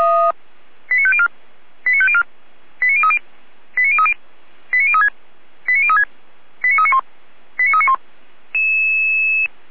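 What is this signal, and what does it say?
Five-tone selective call (ZVEI), heard as quick runs of five tones, tones of about 70 ms. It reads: 89504 89504 89030 89030 89036 89036 89391 89391
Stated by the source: ZVEI five-tone selective call 89042, received by SDR.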